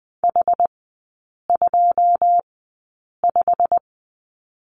Morse code, heard as a keyed tone, about 20 wpm, spaced as for 8 wpm, 700 Hz. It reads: H25
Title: 093 Lesson 16e - Letter c - Three Characters 20wpm